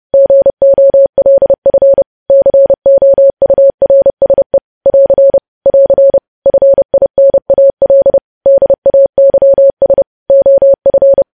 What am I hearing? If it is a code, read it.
Morse code, heard as a keyed tone, about 30 wpm, 550 Hz. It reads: GOLF COURSE <AR> <AR> FINAL DAYS OF
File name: podcast / News headlines-1x-30x30